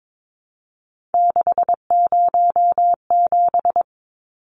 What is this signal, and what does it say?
Morse code, heard as a keyed tone, about 22 wpm, 700 Hz. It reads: 607
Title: Numbers - 3-Digit 22wpm